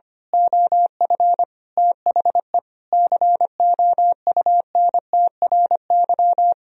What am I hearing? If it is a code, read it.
Morse code, heard as a keyed tone, about 24 wpm, 700 Hz. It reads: OF THE COUNTRY
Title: Speed-Racing - Sets of 5 Words 25wpm